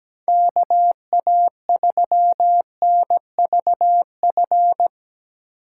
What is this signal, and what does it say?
Morse code, heard as a keyed tone, about 17 wpm, 700 Hz. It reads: KA3NVF
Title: USA Call Signs - 6 Characters - 3x Character Spacing - 17wpm